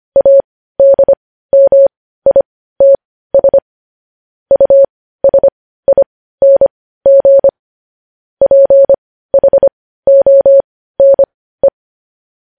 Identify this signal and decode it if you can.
Morse code, heard as a keyed tone, about 25 wpm, spaced as for 15 wpm, 550 Hz. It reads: ADMITS USING PHONE
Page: News Headlines in Morse Code at 15 WPM